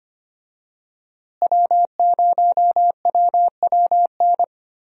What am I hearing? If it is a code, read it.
Morse code, heard as a keyed tone, about 25 wpm, 700 Hz. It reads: W0WWN
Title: USA Call Signs - 5 Characters - Repeated 3x - 25wpm